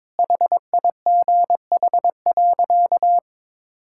Morse code, heard as a keyed tone, about 22 wpm, 700 Hz. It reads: HIGH.
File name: practice / 22wpm / Sentences with 3-Word Phrases - 22wpm